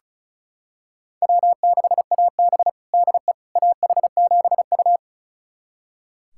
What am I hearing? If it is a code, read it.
Morse code, heard as a keyed tone, about 36 wpm, 700 Hz. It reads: W6AB DE AH7U
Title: USA Callsign DE Callsign - 4 Characters - 35wpm